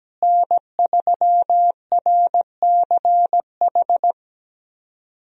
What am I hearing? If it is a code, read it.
Morse code, heard as a keyed tone, about 17 wpm, 700 Hz. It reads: N3RCH
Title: USA Call Signs - 5 Characters - Repeated 3x - 17wpm